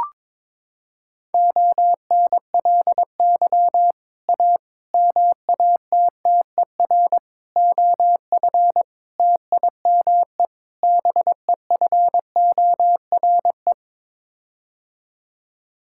Morse code, heard as a keyed tone, about 21 wpm, 700 Hz. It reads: ONLY A MATTER OF TIME BEFORE
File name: practice / 22wpm / Sets of 6 Words - 22wpm